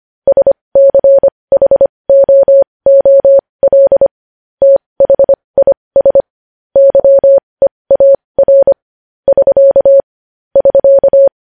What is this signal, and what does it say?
Morse code, heard as a keyed tone, about 25 wpm, spaced as for 20 wpm, 550 Hz. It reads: SCHOOL THIS YEAR <SK> <SK>
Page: news-headlines-in-morse-code-at-20-wpm